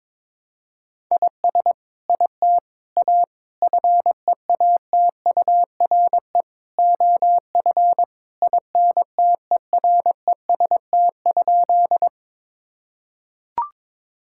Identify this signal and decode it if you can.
Morse code, heard as a keyed tone, about 22 wpm, 700 Hz. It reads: IS IT A FEATURE OF INTEREST?
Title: Sentences from Top 1000 Words - Review Part 5 - 22wpm